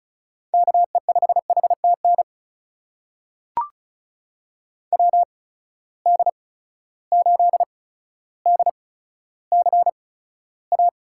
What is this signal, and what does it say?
Morse code, heard as a keyed tone, about 36 wpm, spaced as for 15 wpm, 700 Hz. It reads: KE5HTN WD8DCA
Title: USA Call Signs - 6 Characters - 8x Character Spacing - 35wpm